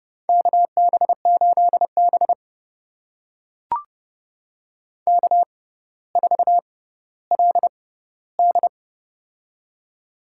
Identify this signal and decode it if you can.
Morse code, heard as a keyed tone, about 32 wpm, spaced as for 16 wpm, 700 Hz. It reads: KB8B K4LD
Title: USA Call Signs - 4 Characters - 6x Character Spacing - 30wpm